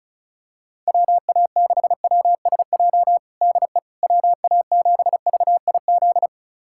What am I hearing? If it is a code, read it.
Morse code, heard as a keyed tone, about 34 wpm, 700 Hz. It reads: WA6WSJ DE WA7VIZ
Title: USA Callsign DE Callsign - 6 Characters - 35wpm